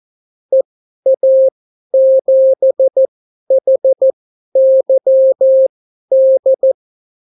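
Morse code, heard as a keyed tone, about 14 wpm, spaced as for 11 wpm, 530 Hz. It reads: EA7HYD
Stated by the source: Example Spanish CW QSO from a Morse learning book, 14 wpm characters with 8 wpm spacing.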